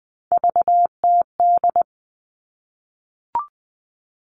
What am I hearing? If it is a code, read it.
Morse code, heard as a keyed tone, about 20 wpm, 700 Hz. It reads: VTD
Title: All Cut Numbers - 3-Digit 20wpm